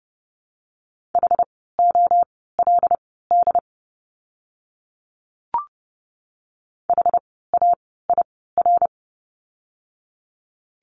Morse code, heard as a keyed tone, about 30 wpm, spaced as for 17 wpm, 700 Hz. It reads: HOLD HAIR
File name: Lesson 17f - Letter u - Words 30wpm